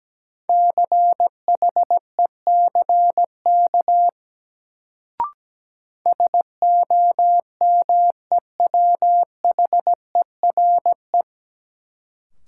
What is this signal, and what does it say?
Morse code, heard as a keyed tone, about 17 wpm, 700 Hz. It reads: CHECK SOMEWHERE
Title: Sentences from Top 1400 Words - New Word Drill Alt - 17wpm